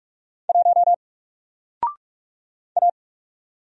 Morse code, heard as a keyed tone, about 44 wpm, 700 Hz. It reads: 1 A
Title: Single Character (Letter:Number:Punctuation:Prosign) 45wpm